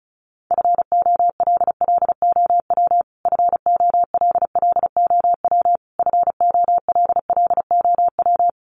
Morse code, heard as a keyed tone, about 34 wpm, 700 Hz. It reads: FOLLOW FOLLOW FOLLOW
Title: Top 300 Words Repeated 35wpm